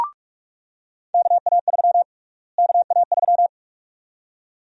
Morse code, heard as a keyed tone, about 42 wpm, 700 Hz. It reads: KA3 KA3
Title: USA Call Sign Prefixes - Repeated 2x - 45wpm